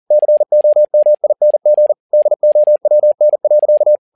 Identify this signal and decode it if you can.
Morse code, heard as a keyed tone, about 40 wpm, 600 Hz. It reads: COMING DOWN.
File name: learning center / tower trouble / tower touble 40wpm